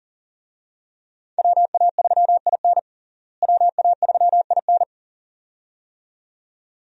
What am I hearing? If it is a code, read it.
Morse code, heard as a keyed tone, about 40 wpm, 700 Hz. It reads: WA3IN WA3IN
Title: USA Call Signs - 5 Characters - Repeated 2x - 40wpm